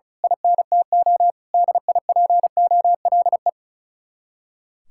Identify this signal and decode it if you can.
Morse code, heard as a keyed tone, about 33 wpm, 700 Hz. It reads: INTO DIPOLE
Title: Speed-Racing Common QSO Phrases - 35wpm